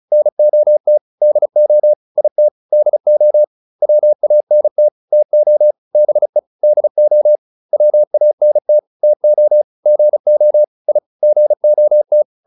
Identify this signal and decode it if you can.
Morse code, heard as a keyed tone, about 34 wpm, 600 Hz. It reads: NOT DO IT DO WANT TO BE DO WANT TO GO I GOT